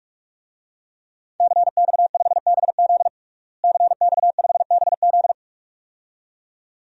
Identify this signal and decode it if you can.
Morse code, heard as a keyed tone, about 41 wpm, 700 Hz. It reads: CX5BZ CX5BZ
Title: DXCC-100 Call Signs - 5 Characters - Repeated 2x - 45wpm